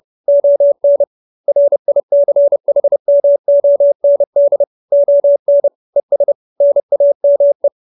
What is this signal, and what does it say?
Morse code, heard as a keyed tone, about 30 wpm, 570 Hz. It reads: ON RICHMOND ON ES NAME